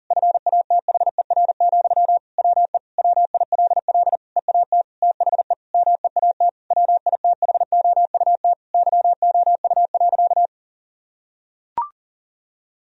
Morse code, heard as a keyed tone, about 37 wpm, 700 Hz. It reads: FATHER, WE WILL EAT THE MEAT WITHOUT YOU.
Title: Sentences from Top 1000 Words - Review Part 3 - 40wpm